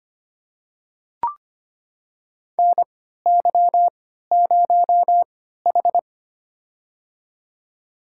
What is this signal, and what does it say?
Morse code, heard as a keyed tone, about 25 wpm, spaced as for 14 wpm, 700 Hz. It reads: NY0H
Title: USA Call Signs - 4 Characters - 3x Character Spacing - 25wpm